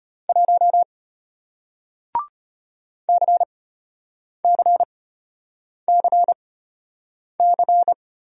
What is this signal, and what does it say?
Morse code, heard as a keyed tone, about 33 wpm, spaced as for 8 wpm, 700 Hz. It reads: 1 CCCC